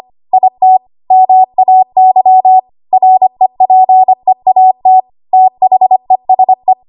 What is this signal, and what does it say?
Morse code, heard as a keyed tone, about 25 wpm, 750 Hz. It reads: IT MAY REPEAT THESE
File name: Morse / 25WPM